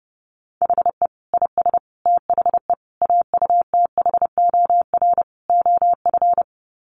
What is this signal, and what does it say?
Morse code, heard as a keyed tone, about 30 wpm, 700 Hz. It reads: HE IS THE AUTHOR OF
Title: Sets of 5 Words Encore2 30wpm